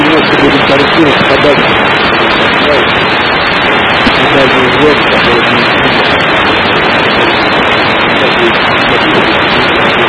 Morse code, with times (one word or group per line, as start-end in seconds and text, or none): none